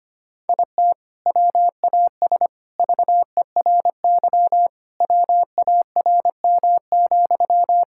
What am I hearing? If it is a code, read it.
Morse code, heard as a keyed tone, about 25 wpm, 700 Hz. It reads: IT WAS VERY WARM,
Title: Sentences from Top 1000 Words - Review Part 5 - 25wpm